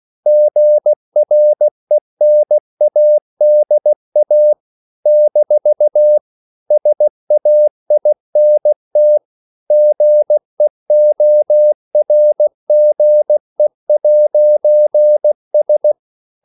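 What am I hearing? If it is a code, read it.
Morse code, heard as a keyed tone, about 16 wpm, 600 Hz. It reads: GRENADA - SAINT GEORGE'S
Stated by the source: CW practice of national capitals at 16 words per minute, from ON6ZQ.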